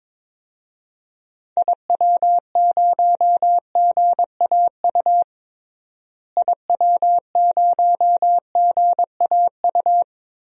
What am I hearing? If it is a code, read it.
Morse code, heard as a keyed tone, about 22 wpm, 700 Hz. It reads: IW0GAU IW0GAU
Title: DXCC-100 Call Signs - 6 Characters - Repeated 3x - 22wpm